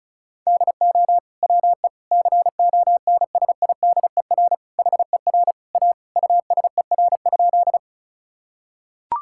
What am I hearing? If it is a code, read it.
Morse code, heard as a keyed tone, about 36 wpm, 700 Hz. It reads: DO WE CONSIDER HER A USER?
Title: Sentences from Top 1000 Words - Review Part 5 - 35wpm